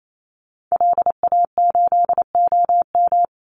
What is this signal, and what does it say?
Morse code, heard as a keyed tone, about 28 wpm, 700 Hz. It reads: LA8OM
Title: Call Signs - 28wpm